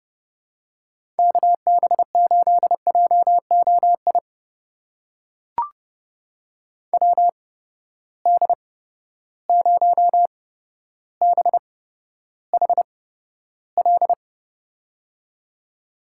Morse code, heard as a keyed tone, about 30 wpm, spaced as for 13 wpm, 700 Hz. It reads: KB8JOI WD0BHL
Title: USA Call Signs - 6 Characters - 8x Character Spacing - 30wpm